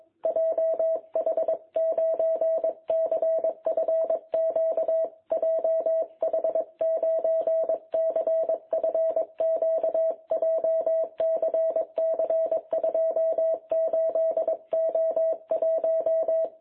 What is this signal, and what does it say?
Morse code, heard as a keyed tone, about 21 wpm, 660 Hz. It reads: JH9CFQ JH9CFQ JCC 28O1
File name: jh8cbh / 2021cw